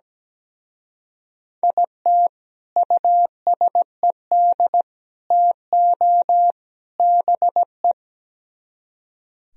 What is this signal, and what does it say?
Morse code, heard as a keyed tone, about 19 wpm, 700 Hz. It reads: IT USED TO BE
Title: Speed-Racing - Sets of 4 Words 17wpm